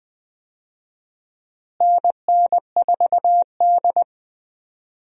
Morse code, heard as a keyed tone, about 20 wpm, 700 Hz. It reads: NN4D
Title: USA Call Signs - 4 Characters - Repeated 3x - 20wpm